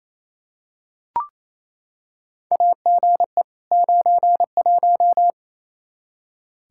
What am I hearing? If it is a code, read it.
Morse code, heard as a keyed tone, about 28 wpm, 700 Hz. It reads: AGE 91
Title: Practice QSO - Exchange 3 - More Info - 28wpm